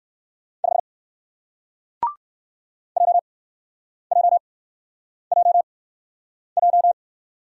Morse code, heard as a keyed tone, about 54 wpm, spaced as for 9 wpm, 700 Hz. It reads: 5 JJJJ
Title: Speed-Racing Single Letter-Number 45wpm